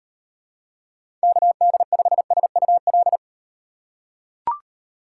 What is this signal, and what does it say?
Morse code, heard as a keyed tone, about 38 wpm, 700 Hz. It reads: KD5SUL